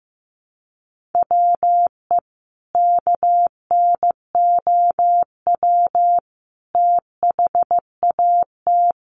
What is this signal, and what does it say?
Morse code, heard as a keyed tone, about 15 wpm, 700 Hz. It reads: WE KNOW THAT
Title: Sentences from Top 1100 Words 15wpm